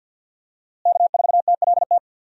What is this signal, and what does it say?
Morse code, heard as a keyed tone, about 51 wpm, 700 Hz. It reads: K4TLT
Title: USA Call Signs - 5 Characters - Repeated 3x - 50wpm